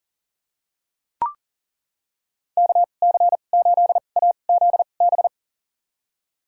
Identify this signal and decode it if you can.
Morse code, heard as a keyed tone, about 38 wpm, spaced as for 28 wpm, 700 Hz. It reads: KC8AZB
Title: USA Call Signs - 6 Characters - 2x Character Spacing - 40wpm